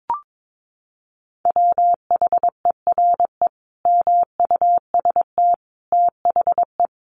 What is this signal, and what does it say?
Morse code, heard as a keyed tone, about 22 wpm, 700 Hz. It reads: WHERE MUST THE
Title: Sentences from Top 1300 Words 22wpm